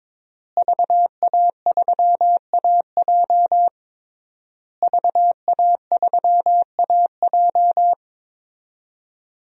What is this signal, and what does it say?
Morse code, heard as a keyed tone, about 22 wpm, 700 Hz. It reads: VA3AJ VA3AJ
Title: DXCC-100 Call Signs - 5 Characters - Repeated 3x - 22wpm